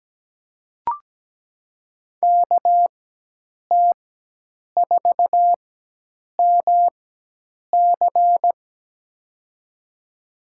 Morse code, heard as a keyed tone, about 17 wpm, spaced as for 8 wpm, 700 Hz. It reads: KT4MC
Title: USA Call Signs - 5 Characters - 4x Character Spacing - 17wpm